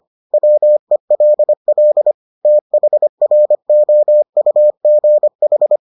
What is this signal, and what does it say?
Morse code, heard as a keyed tone, about 25 wpm, 600 Hz. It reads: WELL THROUGH